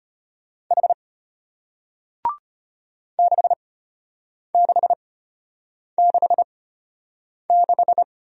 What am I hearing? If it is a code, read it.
Morse code, heard as a keyed tone, about 34 wpm, spaced as for 8 wpm, 700 Hz. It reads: H 6666